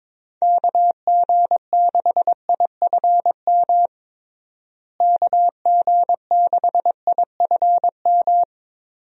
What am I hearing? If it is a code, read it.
Morse code, heard as a keyed tone, about 22 wpm, 700 Hz. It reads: KG6IFM KG6IFM
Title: USA Call Signs - 6 Characters - Repeated 2x - 22wpm